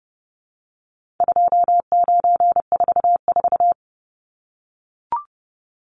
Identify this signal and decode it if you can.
Morse code, heard as a keyed tone, about 30 wpm, 700 Hz. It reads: 2944
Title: Numbers - 4-Digit 30wpm